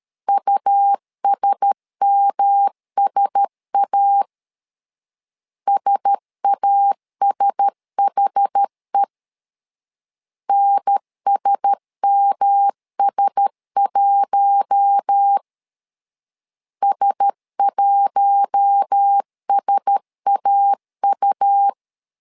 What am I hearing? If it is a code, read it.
Morse code, heard as a keyed tone, about 13 wpm, 800 Hz. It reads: USMSA SASHE NSMS1 S1SAU
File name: LEZIONI / 4 Q1S 12wpm 1802